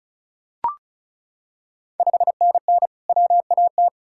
Morse code, heard as a keyed tone, about 36 wpm, 700 Hz. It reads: 5NN WAT
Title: Ontario QSO Party Exchange 35wpm